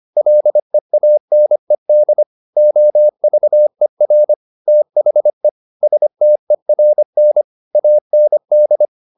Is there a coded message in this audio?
Yes: Morse code from 0.2 to 8.9 s, heard as a keyed tone, about 25 wpm, 600 Hz. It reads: LEANED OVER THE STERN AND